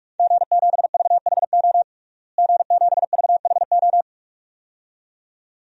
Morse code, heard as a keyed tone, about 46 wpm, 700 Hz. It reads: G7VHO G7VHO
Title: DXCC-100 Call Signs - 5 Characters - Repeated 2x - 45wpm